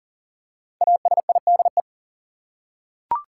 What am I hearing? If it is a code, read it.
Morse code, heard as a keyed tone, about 40 wpm, 700 Hz. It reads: ASIDE
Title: Five Letter Words - 40wpm